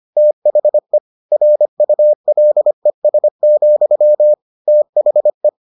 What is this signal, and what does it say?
Morse code, heard as a keyed tone, about 25 wpm, 600 Hz. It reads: THE RULES, THE